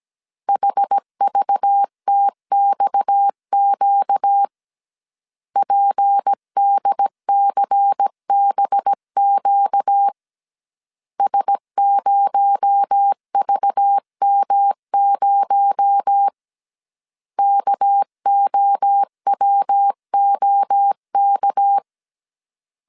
Morse code, heard as a keyed tone, about 17 wpm, 800 Hz. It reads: HVTXQ PDCBQ S0VM0 KOWOK
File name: 16wpm